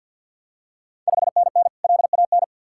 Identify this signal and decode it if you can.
Morse code, heard as a keyed tone, about 51 wpm, 690 Hz. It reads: 5NN LAN
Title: Ontario QSO Party Exchange 50wpm